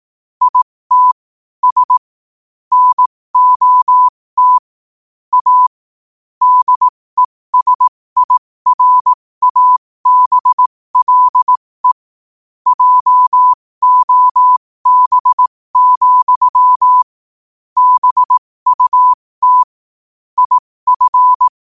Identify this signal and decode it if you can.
Morse code, heard as a keyed tone, about 18 wpm, spaced as for 15 wpm, 1000 Hz. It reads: IT S NOT A DESIRABLE JOB, BUT IF
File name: 15wpm